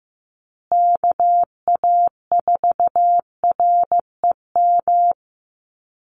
Morse code, heard as a keyed tone, about 15 wpm, 700 Hz. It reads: KA4REM
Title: USA Call Signs Repeated 2x 15wpm